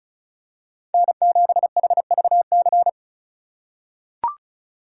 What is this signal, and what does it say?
Morse code, heard as a keyed tone, about 36 wpm, 700 Hz. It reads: N7HVC
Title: USA Call Signs - 3x Character Spacing - 35wpm